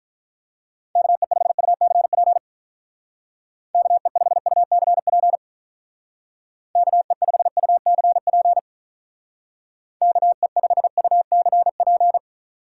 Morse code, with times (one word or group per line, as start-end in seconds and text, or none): 1.0-2.4 KE5UCP
3.7-5.4 KE5UCP
6.8-8.6 KE5UCP
10.0-12.2 KE5UCP